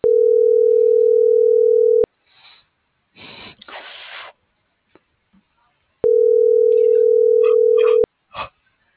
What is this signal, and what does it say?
Ringback tone (North America), 440 + 480 Hz.